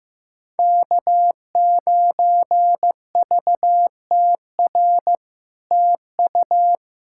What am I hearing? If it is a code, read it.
Morse code, heard as a keyed tone, about 16 wpm, 700 Hz. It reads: K9VTR TU
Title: SOTA QSO Practice 15wpm